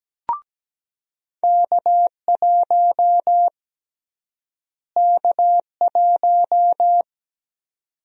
Morse code, heard as a keyed tone, about 17 wpm, 700 Hz. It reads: K1 K1
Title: USA Call Sign Prefixes - Repeated 2x - 17wpm